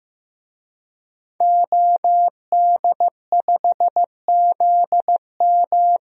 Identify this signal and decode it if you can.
Morse code, heard as a keyed tone, about 15 wpm, 700 Hz. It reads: OD5ZM
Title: DXCC-100 Call Signs - 5 Characters - Repeated 3x - 15wpm